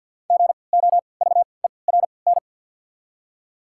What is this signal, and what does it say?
Morse code, heard as a keyed tone, about 53 wpm, spaced as for 29 wpm, 700 Hz. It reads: GOVERN
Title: 3x Character Spacing - Top 500 Words - 50wpm